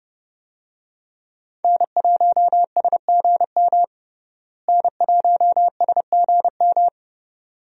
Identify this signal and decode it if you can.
Morse code, heard as a keyed tone, about 30 wpm, 700 Hz. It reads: N1SGM N1SGM
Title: USA Call Signs - 5 Characters - Repeated 2x - 30wpm